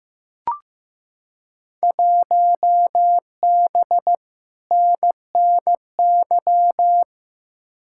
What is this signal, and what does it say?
Morse code, heard as a keyed tone, about 15 wpm, 700 Hz. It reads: 1B NNY
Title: Field Day Exchanges Encore - 15wpm